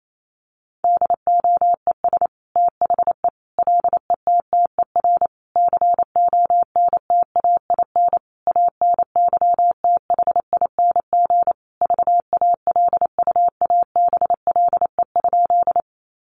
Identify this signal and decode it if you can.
Morse code, heard as a keyed tone, about 28 wpm, 700 Hz. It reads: DOES THE LETTER CONTAIN ANYTHING VALUABLE?
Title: Sentences from Top 1100 Words 28wpm